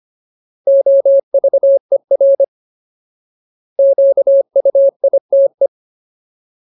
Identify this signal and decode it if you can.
Morse code, heard as a keyed tone, about 25 wpm, 550 Hz. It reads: OVER QUITE
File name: Ultimate Word List / 25WPM Ultimate